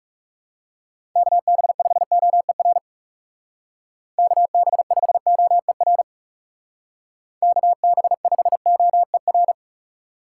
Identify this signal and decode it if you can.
Morse code, heard as a keyed tone, about 40 wpm, 700 Hz. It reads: KB5OER KB5OER KB5OER